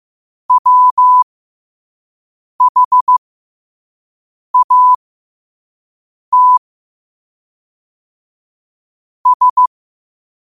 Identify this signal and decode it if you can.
Morse code, heard as a keyed tone, about 15 wpm, spaced as for 5 wpm, 1000 Hz. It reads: WHAT S
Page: 5 WPM morse code quotes for Wed, 30 Jul 2025.